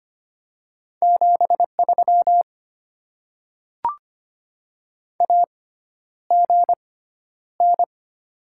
Morse code, heard as a keyed tone, about 25 wpm, spaced as for 11 wpm, 700 Hz. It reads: 73 AGN
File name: Top 20 QSO Elements - 6x Character Spacing - 25wpm